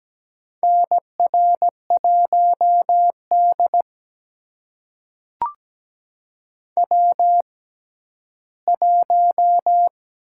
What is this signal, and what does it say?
Morse code, heard as a keyed tone, about 17 wpm, spaced as for 11 wpm, 700 Hz. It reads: NR1D W1
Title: USA Call Signs - 4 Characters - 6x Character Spacing - 17wpm